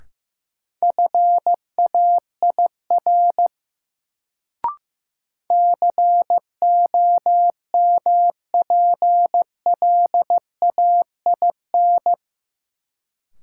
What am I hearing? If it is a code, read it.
Morse code, heard as a keyed tone, about 15 wpm, 700 Hz. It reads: FAIR COMPLAIN
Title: Sentences from Top 1300 Words - New Word Drill Alt - 15wpm